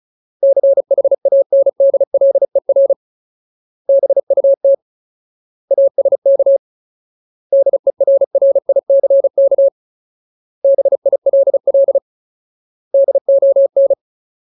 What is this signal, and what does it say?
Morse code, heard as a keyed tone, about 34 wpm, 550 Hz. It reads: CHANDLER BUT ASK DERRICK BILL DON
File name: Ultimate Word List / 35WPM Ultimate